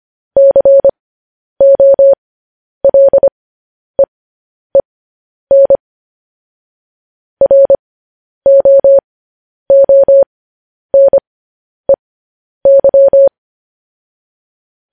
Morse code, heard as a keyed tone, about 25 wpm, spaced as for 10 wpm, 550 Hz. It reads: COLEEN ROONEY
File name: podcast / News headlines-1x-25x10